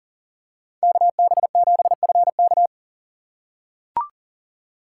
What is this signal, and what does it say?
Morse code, heard as a keyed tone, about 40 wpm, 700 Hz. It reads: KB7FK
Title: USA Call Signs - 6x Character Spacing - 40wpm